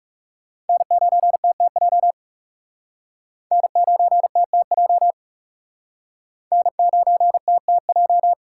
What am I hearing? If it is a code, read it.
Morse code, heard as a keyed tone, about 40 wpm, 700 Hz. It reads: N9TTJ N9TTJ N9TTJ